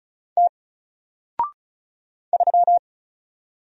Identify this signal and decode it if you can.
Morse code, heard as a keyed tone, about 36 wpm, 700 Hz. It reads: T 3
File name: Single-Letter-Number-35wpm